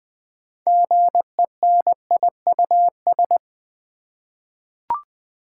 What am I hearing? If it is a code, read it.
Morse code, heard as a keyed tone, about 20 wpm, 700 Hz. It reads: GENIUS
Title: 6 Letter Words - First 3 Letters Given - 20wpm